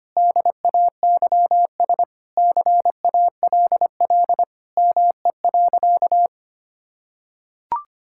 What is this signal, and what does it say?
Morse code, heard as a keyed tone, about 25 wpm, 700 Hz. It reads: DAYS CALL ME.
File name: practice / 25wpm / Sentences with 3-Word Phrases - 25wpm